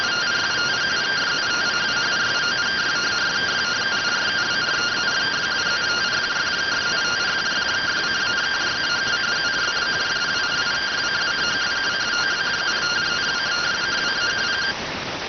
Two-tone teleprinter signal, about 170 Hz shift.